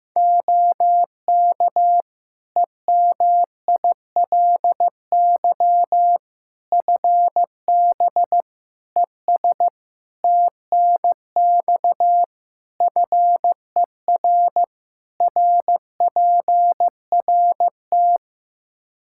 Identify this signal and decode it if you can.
Morse code, heard as a keyed tone, about 15 wpm, 700 Hz. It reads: OK EMILY FB ES TNX FER RPRT
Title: Practice QSO - Exchange 2 - Info - 15wpm